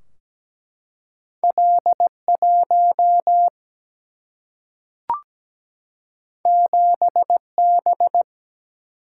Morse code, heard as a keyed tone, about 17 wpm, 700 Hz. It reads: L1 7B
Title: Single-Letter Single-Number 17wpm